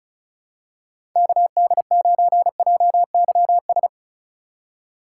Morse code, heard as a keyed tone, about 36 wpm, 700 Hz. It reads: KD9JYS
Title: USA Call Signs - 6 Characters - 3x Character Spacing - 35wpm